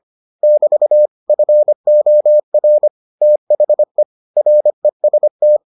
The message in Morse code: <BT> FOR THE REST

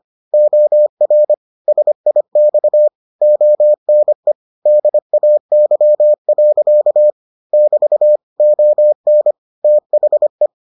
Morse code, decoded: OR SIX ONE DAY. <BT> ON THE

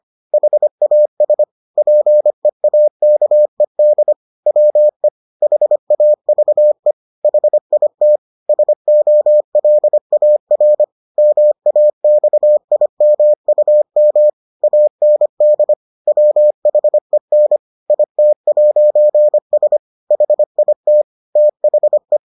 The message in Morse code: HAS PEAKED WE HAVE HIT SOLAR MAXIMUM AND WHEN IT'S HIT THE

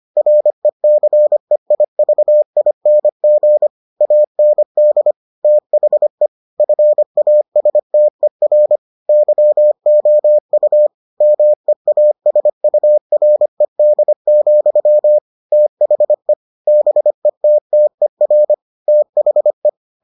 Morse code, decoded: RECEIVING AND THE FASTER YOU MEASURED, THE BETTER THE

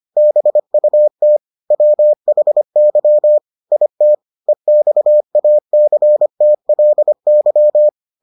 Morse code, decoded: BUT WHY IT EXACTLY

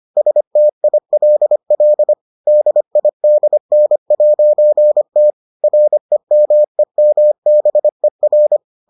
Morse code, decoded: STILL DIDN'T REMEMBER

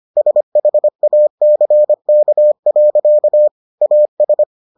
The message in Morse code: SHACK. AS

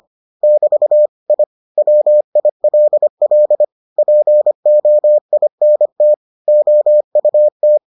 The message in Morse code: <BT> I WILL POINT OUT

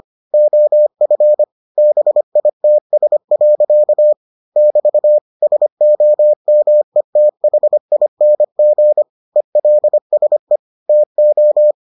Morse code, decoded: OF BITS. <BT> SOMETHING ELSE TO

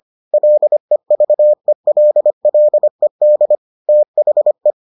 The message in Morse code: LEVELLED THE

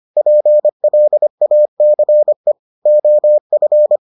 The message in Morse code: PLACE OF

